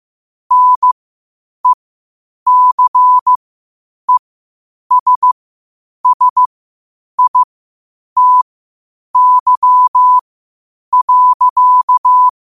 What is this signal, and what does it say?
Morse code, heard as a keyed tone, about 15 wpm, spaced as for 8 wpm, 1000 Hz. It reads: NECESSITY.
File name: qotdfiles / 8wpm